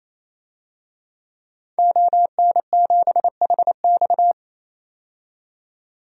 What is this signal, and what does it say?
Morse code, heard as a keyed tone, about 28 wpm, 700 Hz. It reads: ON7HX
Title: DXCC-100 Call Signs - 5 Characters - 28wpm